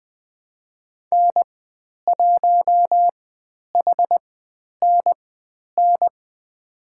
Morse code, heard as a keyed tone, about 20 wpm, spaced as for 10 wpm, 700 Hz. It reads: N1HNN
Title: 102 Lesson 17g - Letter u - Call Signs 20wpm